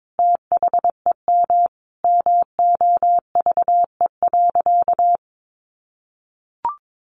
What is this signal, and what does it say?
Morse code, heard as a keyed tone, about 22 wpm, 700 Hz. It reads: THEM MOVE.
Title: A Sound Of Thunder 22wpm